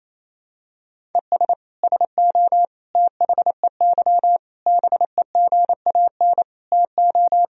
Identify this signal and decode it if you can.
Morse code, heard as a keyed tone, about 28 wpm, 700 Hz. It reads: ES SO THEY BEGAN TO
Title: Sentences from Top 400 Words and Abbreviations 28wpm